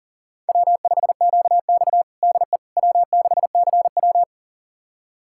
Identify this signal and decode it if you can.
Morse code, heard as a keyed tone, about 40 wpm, 700 Hz. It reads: W5QX DE W6CW
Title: USA Callsign DE Callsign - 4 Characters - 40wpm